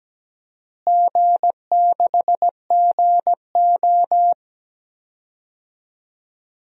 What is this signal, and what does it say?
Morse code, heard as a keyed tone, about 17 wpm, 700 Hz. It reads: G6GO
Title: DXCC-100 Call Signs - 4 Characters - Repeated 3x - 17wpm